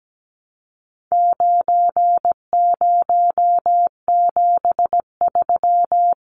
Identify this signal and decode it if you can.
Morse code, heard as a keyed tone, about 17 wpm, 700 Hz. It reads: 9073
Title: Numbers - 4-Digit 17wpm